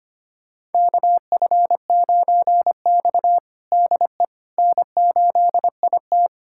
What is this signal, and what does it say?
Morse code, heard as a keyed tone, about 25 wpm, 700 Hz. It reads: KF9X DE N8IT
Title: USA Callsign DE Callsign - 4 Characters - 25wpm